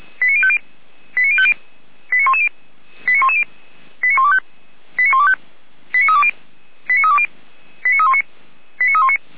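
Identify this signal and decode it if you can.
Five-tone selective call (ZVEI), heard as quick runs of five tones, tones of about 70 ms. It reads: 89050 89050 89110 89110 89125 89125 89320 89320 89319 89319
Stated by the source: ZVEI five-tone selective call received by SDR.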